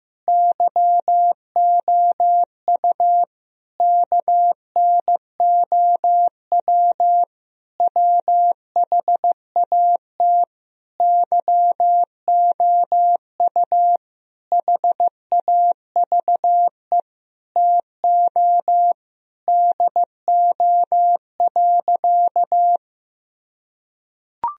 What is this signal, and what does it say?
Morse code, heard as a keyed tone, about 15 wpm, 700 Hz. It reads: YOU KNOW WHAT YOU HAVE TO DO.
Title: Sentences from Top 1000 Words - Review Part 3 - 15wpm